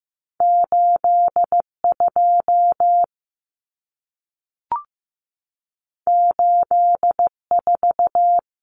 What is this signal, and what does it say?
Morse code, heard as a keyed tone, about 15 wpm, 700 Hz. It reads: 8 2 8 4
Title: Numbers - 2-Digit 15wpm